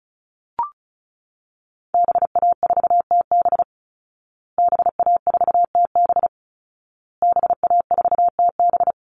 Morse code, heard as a keyed tone, about 34 wpm, 700 Hz. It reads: BA4TB BA4TB BA4TB
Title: Call Signs Repeated 2x - 35wpm